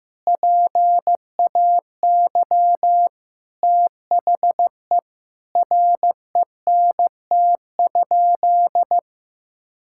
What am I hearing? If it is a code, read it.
Morse code, heard as a keyed tone, about 16 wpm, 700 Hz. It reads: PAY THE RENT?
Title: Sentences from Top 1000 Words - Review Part 6 - 15wpm